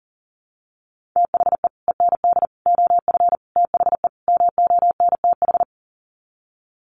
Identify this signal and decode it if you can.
Morse code, heard as a keyed tone, about 40 wpm, 700 Hz. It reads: THE END OF THE MONTH
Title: Sets of 5 Words Encore 40wpm